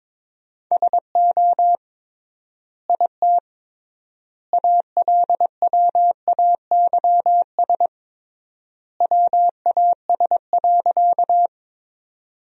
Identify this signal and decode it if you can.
Morse code, heard as a keyed tone, about 21 wpm, 700 Hz. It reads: SO IT ALWAYS WAS.
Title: 3x Word-Spacing - Sentences from Top 300 Words - 22wpm